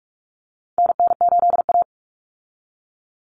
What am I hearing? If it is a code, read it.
Morse code, heard as a keyed tone, about 44 wpm, 700 Hz. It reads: NN8A
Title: Common Cut Numbers - 4-Digit 45wpm